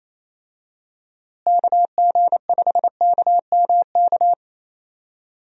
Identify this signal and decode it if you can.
Morse code, heard as a keyed tone, about 28 wpm, 700 Hz. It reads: KG5KMK
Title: USA Call Signs - 6 Characters - Repeated 2x - 28wpm